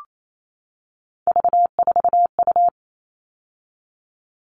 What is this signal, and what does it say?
Morse code, heard as a keyed tone, about 28 wpm, 700 Hz. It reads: V4U
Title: All Cut Numbers - 3-Digit 28wpm